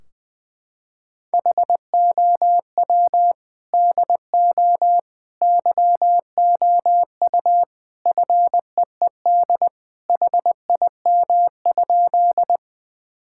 Morse code, heard as a keyed tone, about 20 wpm, 700 Hz. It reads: HOW DO YOU FEED HIM?
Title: Sentences from Top 1600 Words 20wpm